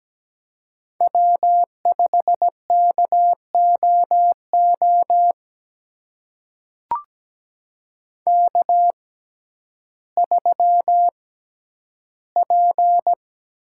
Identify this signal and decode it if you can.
Morse code, heard as a keyed tone, about 17 wpm, spaced as for 10 wpm, 700 Hz. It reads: W5KOO K3P
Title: USA Call Signs - 5 Characters - 6x Character Spacing - 17wpm